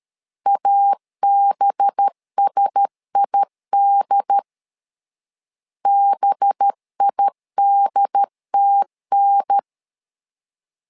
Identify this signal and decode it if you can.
Morse code, heard as a keyed tone, about 13 wpm, 800 Hz. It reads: ABSID BIDTN
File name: LEZIONI / 5 LBVC / 5 LB 12wpm 1906